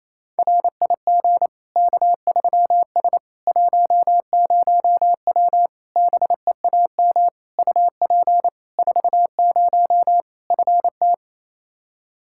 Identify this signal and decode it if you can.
Morse code, heard as a keyed tone, about 27 wpm, 700 Hz. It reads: RIG K3S 10W BEAM UP 40 FT
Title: Practice QSO - Exchange 2 - Info - 28wpm